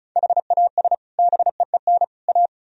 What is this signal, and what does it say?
Morse code, heard as a keyed tone, about 43 wpm, 700 Hz. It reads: HAS BEEN A
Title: Speed-Racing - Sets of 3 Words 35wpm